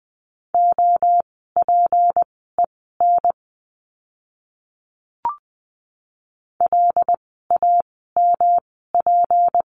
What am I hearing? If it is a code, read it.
Morse code, heard as a keyed tone, about 20 wpm, spaced as for 14 wpm, 700 Hz. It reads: OPEN LAMP
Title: 153 Lesson 25f - Letter p - Words 20wpm